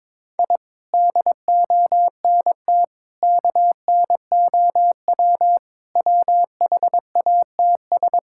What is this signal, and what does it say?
Morse code, heard as a keyed tone, about 22 wpm, 700 Hz. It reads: I DONT KNOW WHATS
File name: practice / 22wpm / Sets of 7 Words - 22wpm